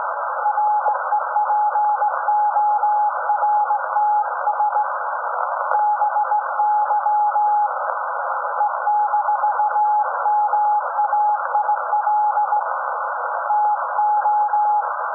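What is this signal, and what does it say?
Morse code, heard as a keyed tone, about 18 wpm, 840 Hz, with noise. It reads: REPORTS BY E-MAIL TO
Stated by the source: SAQ Grimeton Radio, Alexanderson 200 kW alternator on 17.2 kHz, received in Potsdam with a very strong signal on World Radio Day.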